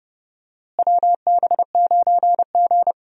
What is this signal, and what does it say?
Morse code, heard as a keyed tone, about 30 wpm, 700 Hz. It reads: WB9G
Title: USA Call Signs - 4 Characters - Repeated 2x - 30wpm